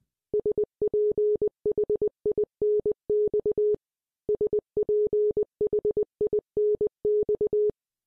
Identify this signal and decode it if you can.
Morse code, heard as a keyed tone, about 20 wpm, 420 Hz. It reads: SPHINX SPHINX